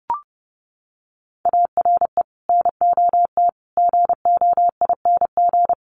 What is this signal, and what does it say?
Morse code, heard as a keyed tone, about 30 wpm, 700 Hz. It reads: ARE NOT GOING
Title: Sets of 5 Words Encore 30wpm